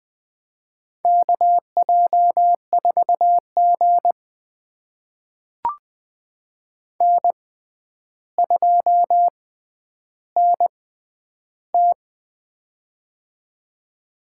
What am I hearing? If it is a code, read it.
Morse code, heard as a keyed tone, about 20 wpm, spaced as for 10 wpm, 700 Hz. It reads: KJ4G N2NT
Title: USA Call Signs - 4 Characters - 6x Character Spacing - 20wpm